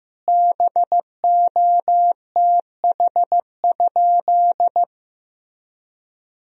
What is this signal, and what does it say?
Morse code, heard as a keyed tone, about 15 wpm, 700 Hz. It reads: BOTH?
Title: Sentences from Top 1000 Words - Review Part 1 - 15wpm